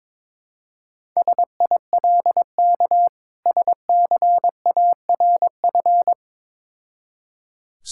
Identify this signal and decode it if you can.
Morse code, heard as a keyed tone, about 22 wpm, 700 Hz. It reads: SILK SCARF
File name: Difficult Words with Hints - 22wpm